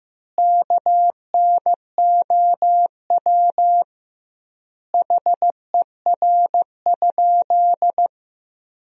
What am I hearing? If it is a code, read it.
Morse code, heard as a keyed tone, about 15 wpm, 700 Hz. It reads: KNOW HER?
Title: 2x Word-Spacing - Sentences from Top 100 Words - 15wpm